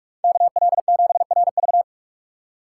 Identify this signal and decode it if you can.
Morse code, heard as a keyed tone, about 46 wpm, 700 Hz. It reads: KL7RV
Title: USA Call Signs - 5 Characters - 4x Character Spacing - 45wpm